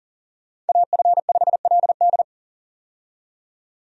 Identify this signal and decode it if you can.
Morse code, heard as a keyed tone, about 40 wpm, 700 Hz. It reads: AF5LD
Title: USA Call Signs - 5 Characters - 40wpm